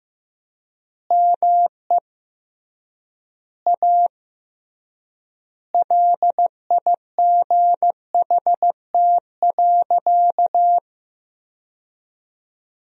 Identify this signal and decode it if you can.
Morse code, heard as a keyed tone, about 15 wpm, 700 Hz. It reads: ME A LIGHT.
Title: 3x Word-Spacing - Sentences from Top 200 Words - 15wpm